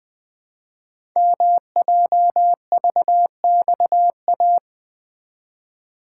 Morse code, heard as a keyed tone, about 20 wpm, 700 Hz. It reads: MJVXA
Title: Code Groups - 5 Letters - 20wpm